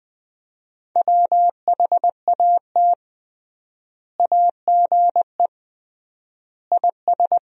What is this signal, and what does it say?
Morse code, heard as a keyed tone, about 20 wpm, 700 Hz. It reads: WHAT AGE IS
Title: 3x Word-Spacing - Sentences from Top 500 Words - 20wpm